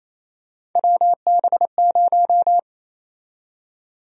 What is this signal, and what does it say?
Morse code, heard as a keyed tone, about 28 wpm, 700 Hz. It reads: WB0
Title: USA Call Sign Prefixes - 28wpm